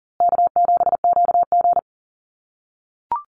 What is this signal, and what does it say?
Morse code, heard as a keyed tone, about 40 wpm, 700 Hz. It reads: K7QG
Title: USA Call Signs Repeated 3x 40wpm